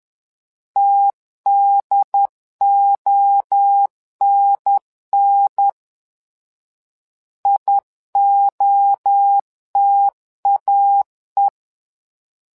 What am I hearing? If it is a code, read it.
Morse code, heard as a keyed tone, about 10 wpm, 800 Hz. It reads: TDONN IOTAE
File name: LEZIONI / 3 MWRO / 3 MWRO 10wpm 2148